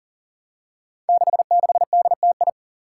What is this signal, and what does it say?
Morse code, heard as a keyed tone, about 40 wpm, 700 Hz. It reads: 66DTI
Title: Five Challenging Letter-Number Combinations 40wpm